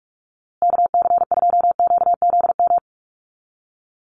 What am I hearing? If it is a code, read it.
Morse code, heard as a keyed tone, about 44 wpm, 700 Hz. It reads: KC2QZM
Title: USA Call Signs Repeated 2x 45wpm